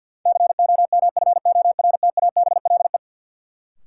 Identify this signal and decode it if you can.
Morse code, heard as a keyed tone, about 52 wpm, 700 Hz. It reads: COMFORTABLE
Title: Sentences from Top 1000 Words - Review Part 7 - New Word Drill - 50wpm